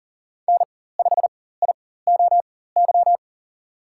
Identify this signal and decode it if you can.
Morse code, heard as a keyed tone, about 40 wpm, spaced as for 19 wpm, 700 Hz. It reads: N5IOY
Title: USA Call Signs - 5 Characters - 4x Character Spacing - 40wpm